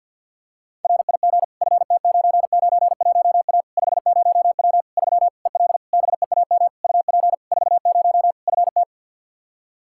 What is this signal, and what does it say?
Morse code, heard as a keyed tone, about 51 wpm, 700 Hz. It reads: RIG FT991A 50W 3 EL BEAM UP 40 FT
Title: Practice QSO - Exchange 2 - Info - 50wpm